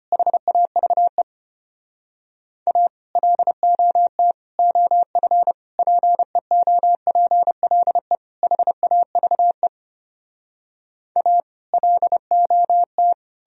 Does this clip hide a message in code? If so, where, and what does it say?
Yes: Morse code from 0.1 to 13.1 s, heard as a keyed tone, about 33 wpm, 700 Hz. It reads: HAVE A LOT OF PEOPLE HAVE A LOT